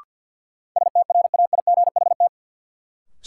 Morse code, heard as a keyed tone, about 51 wpm, 700 Hz. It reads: STRAIGHT
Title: Sentences from Top 1000 Words - New Word Drill - 50wpm